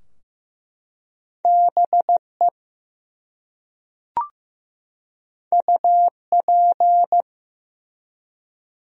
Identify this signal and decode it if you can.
Morse code, heard as a keyed tone, about 15 wpm, 700 Hz. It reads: BE UP
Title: Top 100 Words 15wpm